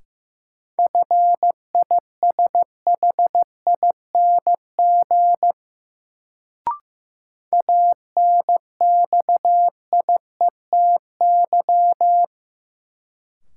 Morse code, heard as a keyed tone, about 15 wpm, 700 Hz. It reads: FISHING ANXIETY
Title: Sentences from Top 1600 Words - New Word Drill Alt - 15wpm